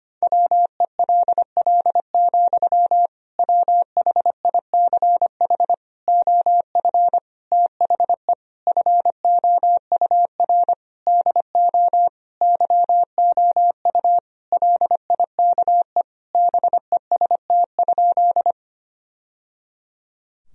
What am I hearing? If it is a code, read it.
Morse code, heard as a keyed tone, about 25 wpm, 700 Hz. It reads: WELL, WHICH OF THE FOUR DO YOU LIKE BEST?
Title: Sentences from Top 1000 Words - Review Part 1 - 25wpm